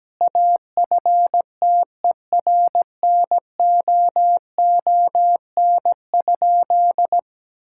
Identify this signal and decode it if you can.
Morse code, heard as a keyed tone, about 17 wpm, 700 Hz. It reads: AFTERNOON?
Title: Sentences from Top 1600 Words 17wpm